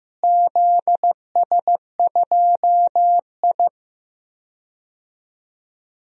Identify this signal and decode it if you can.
Morse code, heard as a keyed tone, about 15 wpm, 700 Hz. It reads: ZS2I
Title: DXCC-100 Call Signs - 4 Characters - Repeated 3x - 15wpm